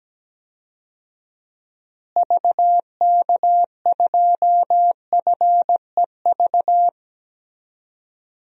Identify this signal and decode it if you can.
Morse code, heard as a keyed tone, about 17 wpm, 700 Hz. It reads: VK2FEV
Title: DXCC-100 Call Signs - 6 Characters - Repeated 2x - 17wpm